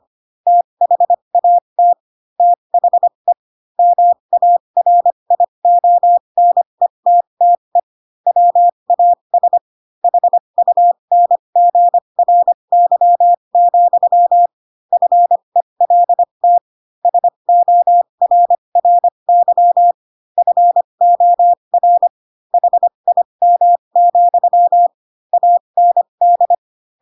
Morse code, heard as a keyed tone, about 25 wpm, spaced as for 22 wpm, 700 Hz. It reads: THAT THE MARIONETTE WAS HUNGRY, FELT SORRY FOR HIM, AND
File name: Pinocchio / Pinocchio ch7 18 25 ik1hge